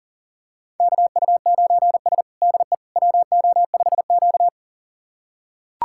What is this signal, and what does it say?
Morse code, heard as a keyed tone, about 40 wpm, 700 Hz. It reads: KU9S DE WO5Q
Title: USA Callsign DE Callsign - 4 Characters - 40wpm